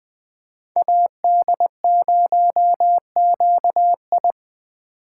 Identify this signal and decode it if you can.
Morse code, heard as a keyed tone, about 20 wpm, 700 Hz. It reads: AD0QI
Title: USA Call Signs - 5 Characters - 4x Character Spacing - 20wpm